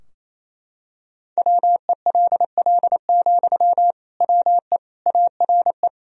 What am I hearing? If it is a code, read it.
Morse code, heard as a keyed tone, about 27 wpm, 700 Hz. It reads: WELL, WE ARE
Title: Sentences from Top 1000 Words - Review Part 4 - 28wpm